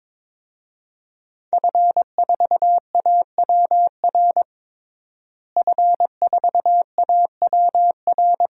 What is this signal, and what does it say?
Morse code, heard as a keyed tone, about 22 wpm, 700 Hz. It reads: F4AWR F4AWR
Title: DXCC-100 Call Signs - 5 Characters - Repeated 2x - 22wpm